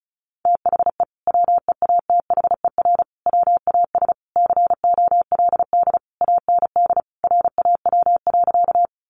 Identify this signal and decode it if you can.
Morse code, heard as a keyed tone, about 34 wpm, 700 Hz. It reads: THE WEATHER WAS COLD AND RAW.
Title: Sentences from Top 1500 Words 35wpm